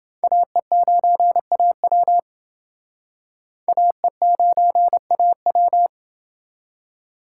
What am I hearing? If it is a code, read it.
Morse code, heard as a keyed tone, about 28 wpm, 700 Hz. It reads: AE9AW AE9AW